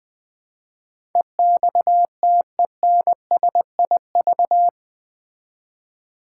Morse code, heard as a keyed tone, about 20 wpm, 700 Hz. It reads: EXTENSIV
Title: Build A Word - 4 Letters At A Time - 11-Letter Words - 20wpm